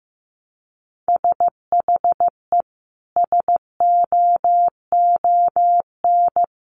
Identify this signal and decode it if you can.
Morse code, heard as a keyed tone, about 15 wpm, 700 Hz. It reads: SHE SOON
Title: Sentences from Top 1100 Words 15wpm